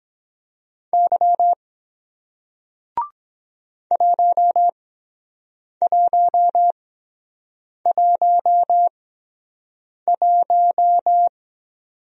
Morse code, heard as a keyed tone, about 23 wpm, spaced as for 7 wpm, 700 Hz. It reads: Y 1111